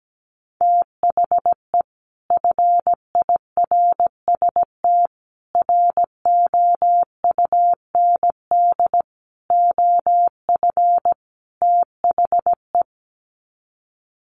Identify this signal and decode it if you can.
Morse code, heard as a keyed tone, about 17 wpm, 700 Hz. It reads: THE FIRST ROUND OF THE
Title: Sets of 5 Words Encore 17wpm